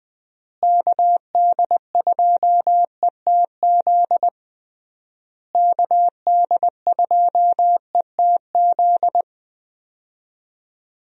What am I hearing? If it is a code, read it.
Morse code, heard as a keyed tone, about 20 wpm, 700 Hz. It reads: KD2ETZ KD2ETZ
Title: USA Call Signs - 6 Characters - Repeated 3x - 20wpm